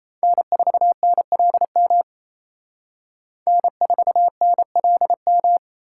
Morse code, heard as a keyed tone, about 35 wpm, 700 Hz. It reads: N4NLM N4NLM